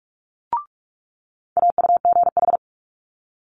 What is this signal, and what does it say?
Morse code, heard as a keyed tone, about 46 wpm, 700 Hz. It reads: AUGH
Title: Multi-letter Phonemes Quick Fire 45wpm